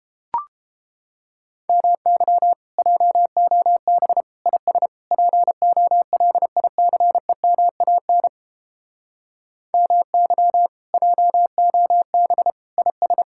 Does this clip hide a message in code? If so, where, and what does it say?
Yes: Morse code from 1.7 to 13.2 s, heard as a keyed tone, about 31 wpm, 700 Hz. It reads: MY JOB IS POLICEMAN MY JOB IS